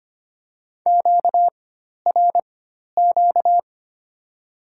Morse code, heard as a keyed tone, about 25 wpm, spaced as for 12 wpm, 700 Hz. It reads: QRQ
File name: Top 90 QSO Elements - 4x Character Spacing - 25wpm